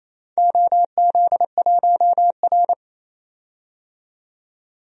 Morse code, heard as a keyed tone, about 28 wpm, 700 Hz. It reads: OZ1R
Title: DXCC-100 Call Signs - 4 Characters - 28wpm